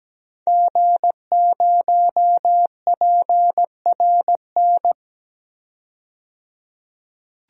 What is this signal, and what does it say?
Morse code, heard as a keyed tone, about 17 wpm, 700 Hz. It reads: G0PRN
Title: DXCC-100 Call Signs - 5 Characters - 17wpm